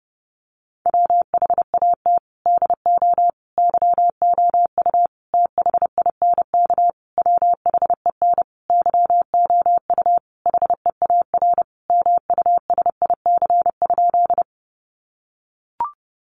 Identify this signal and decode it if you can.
Morse code, heard as a keyed tone, about 30 wpm, 700 Hz. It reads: WHAT DO YOU THINK WHEN YOU HEAR MUSIC?
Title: Sentences from Top 300 Words 30wpm